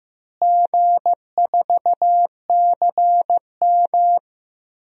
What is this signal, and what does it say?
Morse code, heard as a keyed tone, about 16 wpm, 700 Hz. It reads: G4CM
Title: DXCC-100 Call Signs - 4 Characters - Repeated 3x - 15wpm